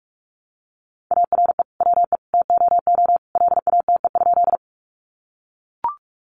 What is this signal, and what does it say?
Morse code, heard as a keyed tone, about 44 wpm, 700 Hz. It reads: ARE WE TOO LATE?
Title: Sentences from Top 300 Words 45wpm